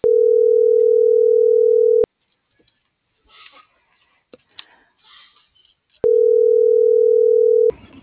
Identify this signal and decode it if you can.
Ringback tone (North America), 440 + 480 Hz.